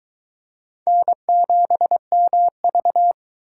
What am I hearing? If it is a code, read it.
Morse code, heard as a keyed tone, about 23 wpm, 700 Hz. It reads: N7MV